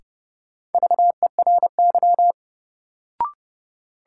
Morse code, heard as a keyed tone, about 30 wpm, 700 Hz. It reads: VERY
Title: Sets of 2 Words - New Word Drill - 30wpm